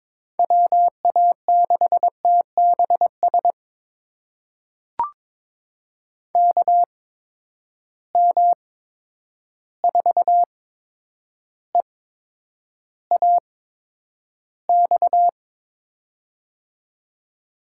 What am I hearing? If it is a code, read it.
Morse code, heard as a keyed tone, about 22 wpm, spaced as for 9 wpm, 700 Hz. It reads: WA6TBS KM4EAX
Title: USA Call Signs - 6 Characters - 8x Character Spacing - 22wpm